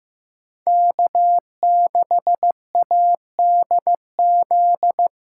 Morse code, heard as a keyed tone, about 15 wpm, 700 Hz. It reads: K6ADZ
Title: USA Call Signs - 5 Characters - 6x Character Spacing - 15wpm